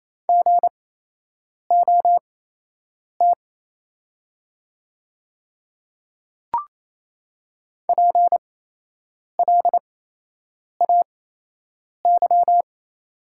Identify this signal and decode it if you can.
Morse code, heard as a keyed tone, about 28 wpm, spaced as for 8 wpm, 700 Hz. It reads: GOT PLAY
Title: 8x Character Spacing - Top 300 Words - 28wpm